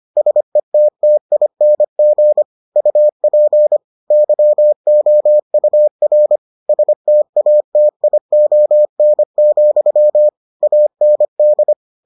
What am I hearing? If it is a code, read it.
Morse code, heard as a keyed tone, about 25 wpm, 600 Hz. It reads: SETTING UP YOUR STATION, AND